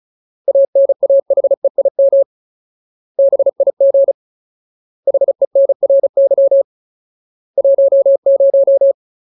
Morse code, heard as a keyed tone, about 34 wpm, 550 Hz. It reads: ANAHEIM BIG HENRY 10
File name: Ultimate Word List / 35WPM Ultimate